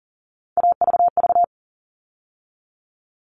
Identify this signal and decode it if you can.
Morse code, heard as a keyed tone, about 40 wpm, 700 Hz. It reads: AVV
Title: All Cut Numbers - 3-Digit 40wpm